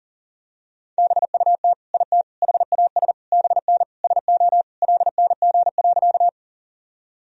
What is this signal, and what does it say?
Morse code, heard as a keyed tone, about 40 wpm, 700 Hz. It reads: BUT IT HAS BN SO LNG.
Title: Sentences from Top 100 Words and Abbreviations 40wpm